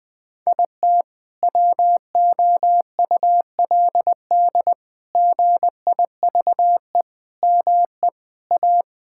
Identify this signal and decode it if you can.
Morse code, heard as a keyed tone, about 20 wpm, 700 Hz. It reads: IT WOULD GIVE ME A
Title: Sentences from Top 1600 Words 20wpm